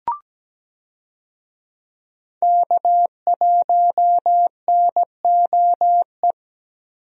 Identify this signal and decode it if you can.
Morse code, heard as a keyed tone, about 17 wpm, 700 Hz. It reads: K1NOE
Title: USA Call Signs - 5 Characters - 17wpm